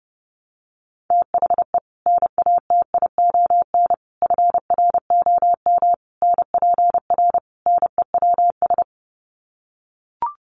Morse code, heard as a keyed tone, about 30 wpm, 700 Hz. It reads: THE NATION FROM NPR NEWS
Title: Sets of 5 Words 30wpm